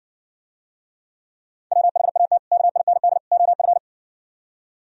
Morse code, heard as a keyed tone, about 59 wpm, 700 Hz. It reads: WHAT KIND OF